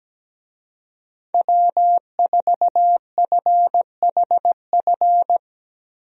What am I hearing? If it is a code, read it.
Morse code, heard as a keyed tone, about 17 wpm, 700 Hz. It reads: W4FHF
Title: USA Call Signs - 5 Characters - Repeated 3x - 17wpm